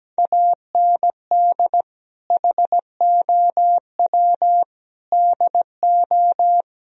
Morse code, heard as a keyed tone, about 17 wpm, 700 Hz. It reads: AND HOW DO
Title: Sentences from Top 1000 Words - Review Part 4 - 17wpm